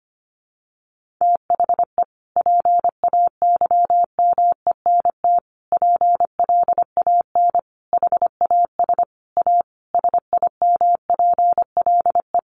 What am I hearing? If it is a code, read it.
Morse code, heard as a keyed tone, about 25 wpm, 700 Hz. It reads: THE PAYMENT PLAN HAS A SIMPLE
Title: Sentences from Top 1500 Words 25wpm